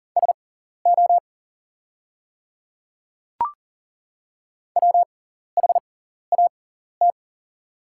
Morse code, heard as a keyed tone, about 38 wpm, spaced as for 14 wpm, 700 Hz. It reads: SO WHAT
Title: 6x Character Spacing - Top 200 Words - 40wpm